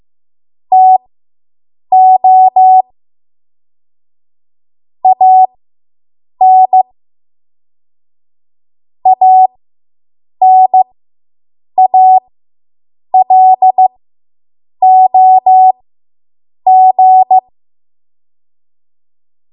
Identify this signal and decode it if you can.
Morse code, heard as a keyed tone, about 15 wpm, spaced as for 7 wpm, 750 Hz. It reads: TO AN ANALOG